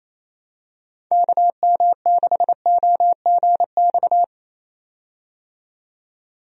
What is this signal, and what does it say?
Morse code, heard as a keyed tone, about 27 wpm, 700 Hz. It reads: KM6OGX
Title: USA Call Signs - 6 Characters - 28wpm